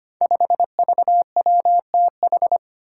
Morse code, heard as a keyed tone, about 25 wpm, 700 Hz. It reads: 5VWTH
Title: Five Challenging Letter-Number Combinations 25wpm